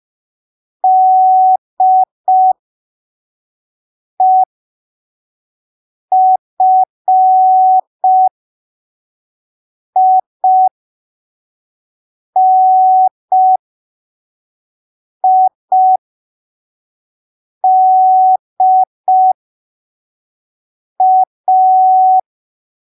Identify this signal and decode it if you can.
Morse code, heard as a keyed tone, about 5 wpm, spaced as for 3 wpm, 750 Hz. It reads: DEFINIDA
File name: arquivos / texto / texto 18 05ppm